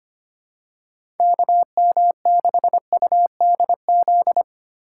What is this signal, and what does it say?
Morse code, heard as a keyed tone, about 25 wpm, 700 Hz. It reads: KM6UDZ
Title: USA Call Signs - 8x Character Spacing - 25wpm